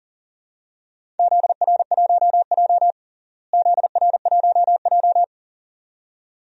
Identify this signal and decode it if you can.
Morse code, heard as a keyed tone, about 40 wpm, 700 Hz. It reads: ZR1J ZR1J
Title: DXCC-100 Call Signs - 4 Characters - Repeated 2x - 40wpm